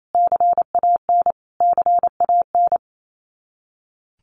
Morse code, heard as a keyed tone, about 28 wpm, 700 Hz. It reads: CAN CAN
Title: Top 200 Words Repeated 28wpm